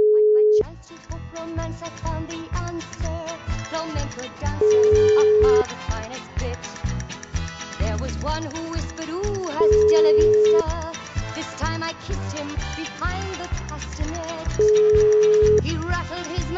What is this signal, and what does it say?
Ringback tone (Europe, 425 Hz), with speech.